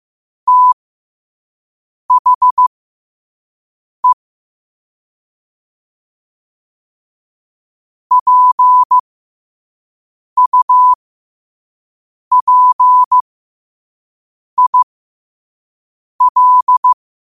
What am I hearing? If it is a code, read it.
Morse code, heard as a keyed tone, about 15 wpm, spaced as for 5 wpm, 1000 Hz. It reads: THE PUPIL
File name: qotdfiles / 5wpm